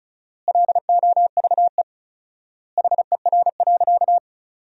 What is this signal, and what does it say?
Morse code, heard as a keyed tone, about 34 wpm, 700 Hz. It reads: LOVE HER.
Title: 4x Word-Spacing - Sentences from Top 400 Words - 35wpm